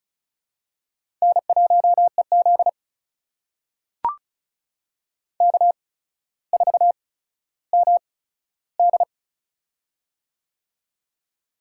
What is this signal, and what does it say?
Morse code, heard as a keyed tone, about 36 wpm, spaced as for 15 wpm, 700 Hz. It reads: N1EZ K4MD
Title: USA Call Signs - 4 Characters - 8x Character Spacing - 35wpm